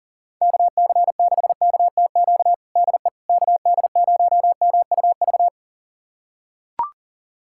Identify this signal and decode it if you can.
Morse code, heard as a keyed tone, about 40 wpm, 700 Hz. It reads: KC6KTQ DE KD0MUV
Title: USA Callsign DE Callsign 40wpm